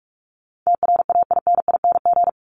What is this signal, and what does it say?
Morse code, heard as a keyed tone, about 44 wpm, 700 Hz. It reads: TRAINING
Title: Top 501-750 Words 45wpm